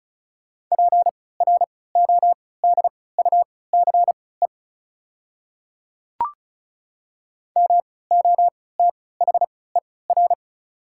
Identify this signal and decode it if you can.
Morse code, heard as a keyed tone, about 36 wpm, spaced as for 20 wpm, 700 Hz. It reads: PRODUCE MOTHER
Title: 3x Character Spacing - Top 500 Words - 35wpm